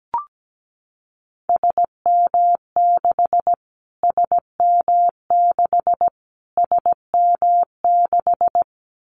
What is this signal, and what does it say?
Morse code, heard as a keyed tone, about 17 wpm, 700 Hz. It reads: SM6 SM6 SM6
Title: Callsign Prefixes for Top 25 DXCC - Repeated 3x - 17wpm